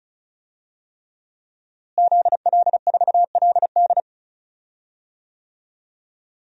Morse code, heard as a keyed tone, about 35 wpm, 700 Hz. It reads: ZL4LD
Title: DXCC-100 Call Signs - 5 Characters - Repeated 3x - 35wpm